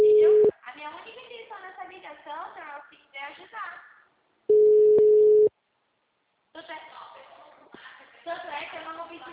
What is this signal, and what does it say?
Ringback tone (Europe, 425 Hz), with speech.
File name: out- -2003- - - .9096